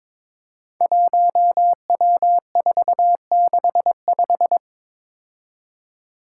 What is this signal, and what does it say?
Morse code, heard as a keyed tone, about 21 wpm, 700 Hz. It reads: 1W465
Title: Five Challenging Letter-Number Combinations 22wpm